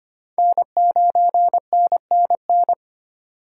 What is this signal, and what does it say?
Morse code, heard as a keyed tone, about 33 wpm, 700 Hz. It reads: N9NNN